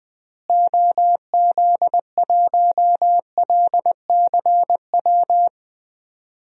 Morse code, heard as a keyed tone, about 20 wpm, 700 Hz. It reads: OZ1LCW
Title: DXCC-100 Call Signs - 6 Characters - Repeated 2x - 20wpm